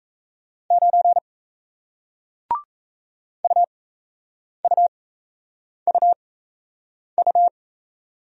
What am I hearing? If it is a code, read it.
Morse code, heard as a keyed tone, about 38 wpm, spaced as for 8 wpm, 700 Hz. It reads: 9 UUUU